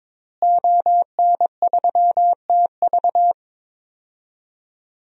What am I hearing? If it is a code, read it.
Morse code, heard as a keyed tone, about 22 wpm, 700 Hz. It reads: ON3TV
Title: DXCC-100 Call Signs - 5 Characters - Repeated 3x - 22wpm